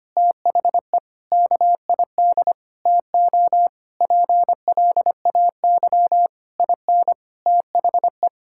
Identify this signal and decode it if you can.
Morse code, heard as a keyed tone, about 25 wpm, 700 Hz. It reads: THE KID TO PLAY IN THE
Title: Sentences from Top 1600 Words 25wpm